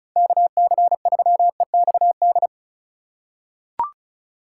Morse code, heard as a keyed tone, about 34 wpm, 700 Hz. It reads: KC3EXD
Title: USA Call Signs - 3x Character Spacing - 35wpm